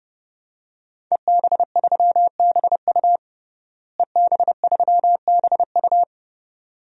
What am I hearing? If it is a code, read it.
Morse code, heard as a keyed tone, about 32 wpm, 700 Hz. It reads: EB3BU EB3BU
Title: DXCC-100 Call Signs - 5 Characters - Repeated 2x - 30wpm